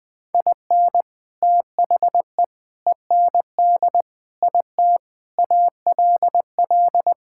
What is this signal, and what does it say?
Morse code, heard as a keyed tone, about 20 wpm, 700 Hz. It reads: IN THE END IT ALL
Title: Sentences with 3-Word Phrases - 20wpm